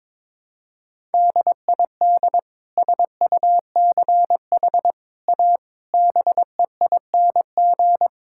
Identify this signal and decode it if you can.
Morse code, heard as a keyed tone, about 22 wpm, 700 Hz. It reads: DID SUCH A BEING
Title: Sentences from Top 1000 Words - Review Part 5 - 22wpm